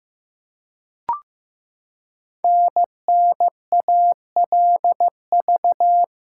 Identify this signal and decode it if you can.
Morse code, heard as a keyed tone, about 15 wpm, 700 Hz. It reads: NNALV
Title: Code Groups - 5 Letter:Numbers - 15wpm